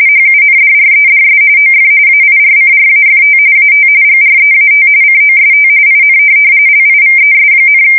RTTY, about 45.45 baud. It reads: HUNDREDS OF AIRCRAFT AT GREAT COST IN LIVES,
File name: fsk170 / USS-New-Jersey